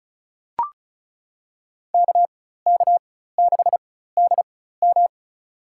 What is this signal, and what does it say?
Morse code, heard as a keyed tone, about 36 wpm, spaced as for 16 wpm, 700 Hz. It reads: KK6DM
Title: USA Call Signs - 4x Character Spacing - 35wpm